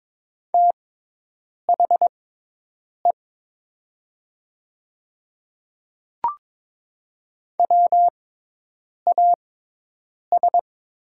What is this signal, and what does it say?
Morse code, heard as a keyed tone, about 23 wpm, spaced as for 8 wpm, 700 Hz. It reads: THE WAS
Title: 6x Character Spacing - Top 100 Words - 22wpm